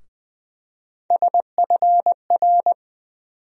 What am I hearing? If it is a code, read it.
Morse code, heard as a keyed tone, about 20 wpm, 700 Hz. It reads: SFR
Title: USA Call Sign Postfixes - 3 Letters 20wpm